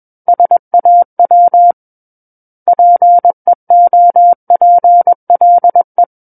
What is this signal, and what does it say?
Morse code, heard as a keyed tone, about 21 wpm, 700 Hz. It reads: SAW PEOPLE
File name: Book One / 12-Chapter 12 - 21 WPM